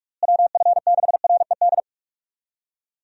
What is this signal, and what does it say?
Morse code, heard as a keyed tone, about 46 wpm, 700 Hz. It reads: WF6RED
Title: USA Call Signs - 6 Characters - Repeated 3x - 45wpm